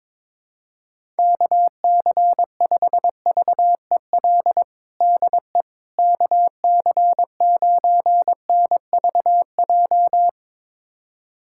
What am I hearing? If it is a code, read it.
Morse code, heard as a keyed tone, about 21 wpm, 700 Hz. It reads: KC5VEL DE KC9NVJ
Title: USA Callsign DE Callsign - 6 Characters - 22wpm